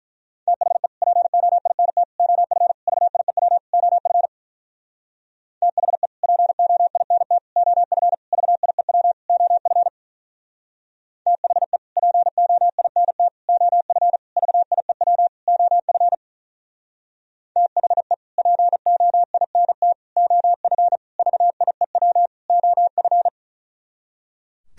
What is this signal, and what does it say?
Morse code, heard as a keyed tone, about 46 wpm, 700 Hz. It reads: THE POINT OF VIEW OF THE POINT OF VIEW OF THE POINT OF VIEW OF THE POINT OF VIEW OF